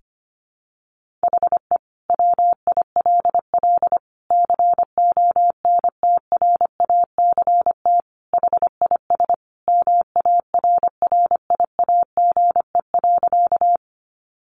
Morse code, heard as a keyed tone, about 25 wpm, 700 Hz. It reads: HE WILL CONTRACT HIS MARRIAGE.
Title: Sentences from Top 1400 Words 25wpm